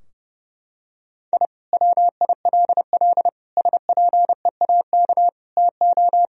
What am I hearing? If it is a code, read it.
Morse code, heard as a keyed tone, about 30 wpm, 700 Hz. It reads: I WILL SPEAK TO
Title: Sentences from Top 1000 Words - Review Part 2 - 30wpm